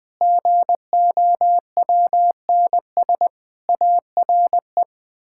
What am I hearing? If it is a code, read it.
Morse code, heard as a keyed tone, about 20 wpm, 700 Hz. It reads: GOWNS ARE
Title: Pangram Challenge v2 - 20wpm